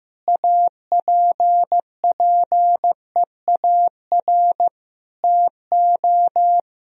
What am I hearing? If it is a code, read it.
Morse code, heard as a keyed tone, about 16 wpm, 700 Hz. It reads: APPEAR TO
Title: Sentences from Top 1000 Words - Review Part 2 - 15wpm